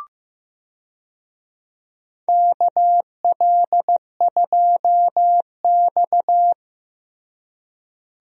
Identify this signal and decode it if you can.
Morse code, heard as a keyed tone, about 15 wpm, 700 Hz. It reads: KL2X
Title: USA Call Signs - 4 Characters - Repeated 3x - 15wpm